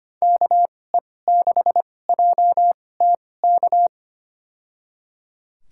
Morse code, heard as a keyed tone, about 25 wpm, spaced as for 18 wpm, 700 Hz. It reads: KE6JTK
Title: USA Call Signs - 6 Characters - 2x Character Spacing - 25wpm